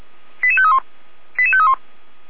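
Five-tone selective call (ZVEI), heard as quick runs of five tones, tones of about 70 ms. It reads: 88531 88531
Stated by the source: ZVEI five-tone selective call received on an SDR, channel 455.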